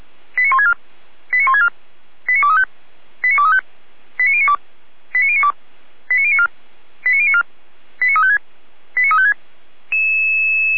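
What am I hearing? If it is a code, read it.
Five-tone selective call (ZVEI), heard as quick runs of five tones, tones of about 70 ms. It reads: 89164 89164 89235 89235 89092 89092 89094 89094 89356 89356
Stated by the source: ZVEI five-tone selective call received on SDR.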